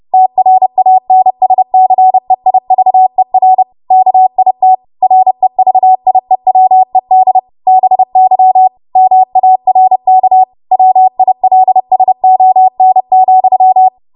Morse code, heard as a keyed tone, about 30 wpm, 750 Hz. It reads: TRANSCEIVER KIT REVIEWED BY MARK WILSON,